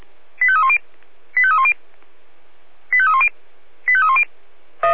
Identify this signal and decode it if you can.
Five-tone selective call (ZVEI), heard as quick runs of five tones, tones of about 70 ms. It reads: 85310 85310 85310 85310
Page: /496/SDR/ZVEI/